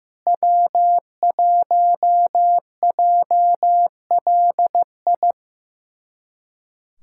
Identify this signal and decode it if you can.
Morse code, heard as a keyed tone, about 16 wpm, 700 Hz. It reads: W1JLI
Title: USA Callsign DE Callsign - 5 Characters - 15wpm